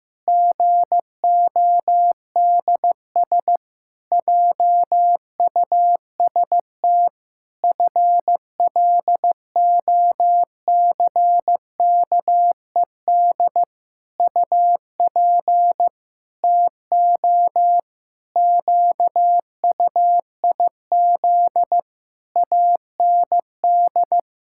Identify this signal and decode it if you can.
Morse code, heard as a keyed tone, about 15 wpm, 700 Hz. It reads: GODS JUST FLOCKED UP TO QUIZ AND
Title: Pangram Challenge v2 - 15wpm